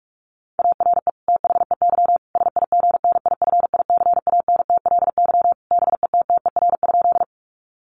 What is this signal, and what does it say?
Morse code, heard as a keyed tone, about 44 wpm, 700 Hz. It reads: ARE THEY SIGNIFICANTLY BETTER?
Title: Sentences from Top 1500 Words 45wpm